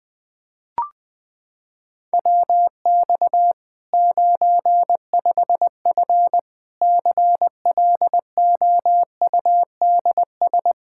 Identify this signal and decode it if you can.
Morse code, heard as a keyed tone, about 21 wpm, 700 Hz. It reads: WX 95F CLOUDS
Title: K1USN SST QSO Practice 20wpm